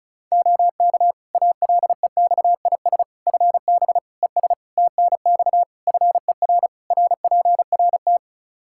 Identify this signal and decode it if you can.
Morse code, heard as a keyed tone, about 34 wpm, 700 Hz. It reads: OK ALEXIS FB ES TNX FER RPRT
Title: Practice QSO - Exchange 2 - Info - 35wpm